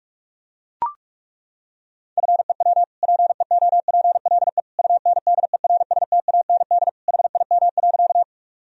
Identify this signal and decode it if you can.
Morse code, heard as a keyed tone, about 42 wpm, 700 Hz. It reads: FEW PEOPLE UNDERSTAND HIM.
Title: Sentences from Top 1000 Words - Review Part 1 - 45wpm